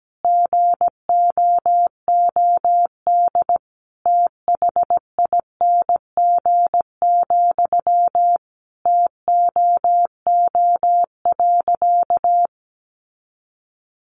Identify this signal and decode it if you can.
Morse code, heard as a keyed tone, about 17 wpm, 700 Hz. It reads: GOOD THING, TOO.
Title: Sentences from Top 200 Words 17wpm